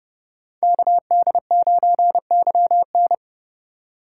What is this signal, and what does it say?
Morse code, heard as a keyed tone, about 36 wpm, 700 Hz. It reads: KD9YN
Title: Speed Racing - USA Call Signs 30wpm